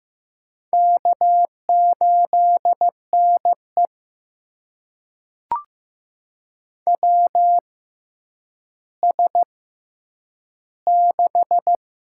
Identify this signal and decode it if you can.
Morse code, heard as a keyed tone, about 15 wpm, spaced as for 9 wpm, 700 Hz. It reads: K8NE WS6
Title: USA Call Signs - 4 Characters - 6x Character Spacing - 15wpm